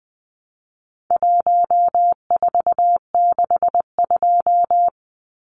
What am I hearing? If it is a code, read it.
Morse code, heard as a keyed tone, about 20 wpm, 700 Hz. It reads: 1462
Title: Numbers - 4-Digit 20wpm